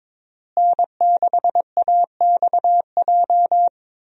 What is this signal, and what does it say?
Morse code, heard as a keyed tone, about 22 wpm, 700 Hz. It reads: N6AXJ
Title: USA Call Signs - 5 Characters - Repeated 2x - 22wpm